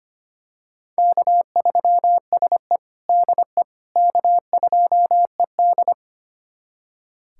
Morse code, heard as a keyed tone, about 25 wpm, 700 Hz. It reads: K3SE DE K2ED
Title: USA Callsign DE Callsign - 4 Characters - 25wpm